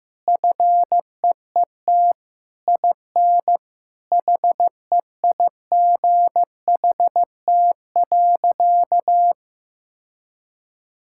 Morse code, heard as a keyed tone, about 15 wpm, 700 Hz. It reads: FEET IN HEIGHT.
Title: Sentences from Top 1600 Words 15wpm